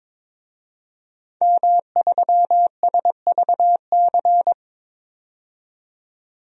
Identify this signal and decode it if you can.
Morse code, heard as a keyed tone, about 22 wpm, 700 Hz. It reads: M3SVC
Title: DXCC-100 Call Signs - 5 Characters - 22wpm